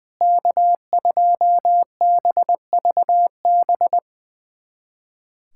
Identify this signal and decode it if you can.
Morse code, heard as a keyed tone, about 20 wpm, 700 Hz. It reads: K2BVB
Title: USA Callsign DE Callsign - 5 Characters - 20wpm